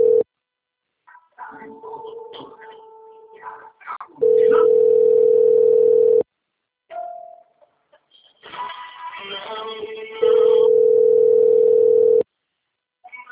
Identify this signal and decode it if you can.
Ringback tone (North America), 440 + 480 Hz, with speech.
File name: out- - - - - .37242